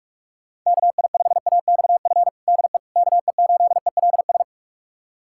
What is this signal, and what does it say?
Morse code, heard as a keyed tone, about 46 wpm, 700 Hz. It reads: KI5AXF DE KE8ELS
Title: USA Callsign DE Callsign - 6 Characters - 45wpm